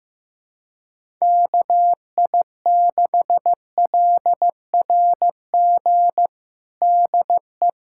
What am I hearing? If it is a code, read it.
Morse code, heard as a keyed tone, about 16 wpm, 700 Hz. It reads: KI6LRG DE
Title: USA Callsign DE Callsign - 6 Characters - 15wpm